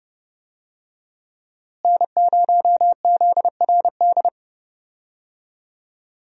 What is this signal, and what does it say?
Morse code, heard as a keyed tone, about 30 wpm, 700 Hz. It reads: N0ZRD
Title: USA Call Signs - 5 Characters - Repeated 2x - 30wpm